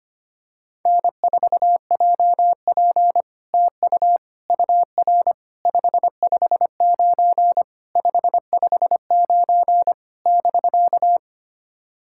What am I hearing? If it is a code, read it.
Morse code, heard as a keyed tone, about 25 wpm, 700 Hz. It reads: N4JP TU UR 559 559 <BK>
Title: SOTA QSO Practice 25wpm